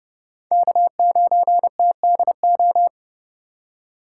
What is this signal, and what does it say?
Morse code, heard as a keyed tone, about 37 wpm, 700 Hz. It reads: K9TDO
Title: Speed Racing - USA 1x3 Call Signs 30wpm